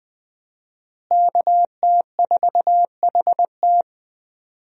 Morse code, heard as a keyed tone, about 20 wpm, 700 Hz. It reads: KT4HT
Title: USA Call Signs - 5 Characters - Repeated 2x - 20wpm